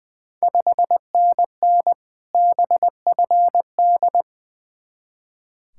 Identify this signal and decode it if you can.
Morse code, heard as a keyed tone, about 20 wpm, 690 Hz. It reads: 5NN BFD
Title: Ontario QSO Party Exchange 20wpm